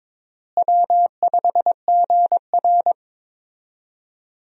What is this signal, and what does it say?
Morse code, heard as a keyed tone, about 21 wpm, 700 Hz. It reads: W5GR
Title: USA Call Signs - 4 Characters - Repeated 3x - 22wpm